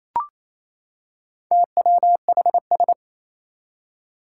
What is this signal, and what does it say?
Morse code, heard as a keyed tone, about 27 wpm, 700 Hz. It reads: TWHS
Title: Four Challenging Letter-Number Combinations 28wpm